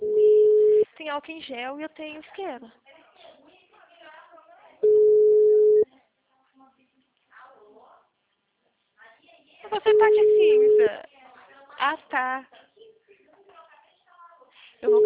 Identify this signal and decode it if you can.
Ringback tone (Europe, 425 Hz), with speech.
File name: out- -2015- - - .56625